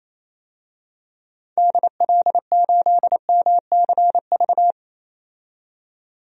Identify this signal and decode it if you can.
Morse code, heard as a keyed tone, about 27 wpm, 700 Hz. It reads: DL8MCV
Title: DXCC-100 Call Signs - 6 Characters - 28wpm